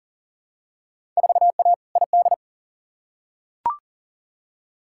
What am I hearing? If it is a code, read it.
Morse code, heard as a keyed tone, about 37 wpm, 700 Hz. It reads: 4A ID
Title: Field Day Exchanges Encore - 40wpm